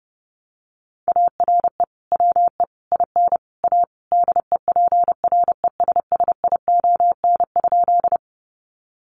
Morse code, heard as a keyed tone, about 30 wpm, 700 Hz. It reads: ARE WE IN A DEPRESSION?
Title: Sentences from Top 1300 Words 30wpm